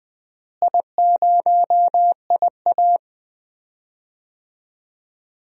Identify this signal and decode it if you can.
Morse code, heard as a keyed tone, about 20 wpm, 700 Hz. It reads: I0IA
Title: DXCC-100 Call Signs - 4 Characters - Repeated 2x - 20wpm